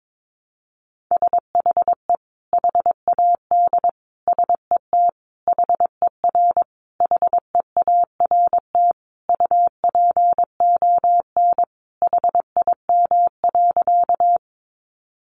Morse code, heard as a keyed tone, about 22 wpm, 700 Hz. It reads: SHE HAD SET HER HEART UPON HIM.
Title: Sentences from Top 1200 Words 22wpm